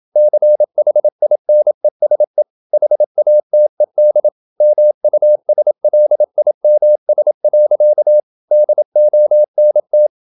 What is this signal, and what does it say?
Morse code, heard as a keyed tone, about 27 wpm, 600 Hz. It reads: CHINESE HATED MUSLIMS. DONT
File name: Never 0027wpm